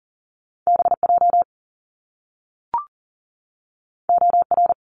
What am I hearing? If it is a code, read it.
Morse code, heard as a keyed tone, about 42 wpm, 700 Hz. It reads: BJ OR
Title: Top Two Letter Combinations in Words 40wpm